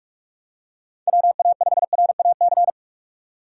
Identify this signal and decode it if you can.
Morse code, heard as a keyed tone, about 44 wpm, 700 Hz. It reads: WA5RAC
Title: USA Call Signs - 6 Characters - 6x Character Spacing - 45wpm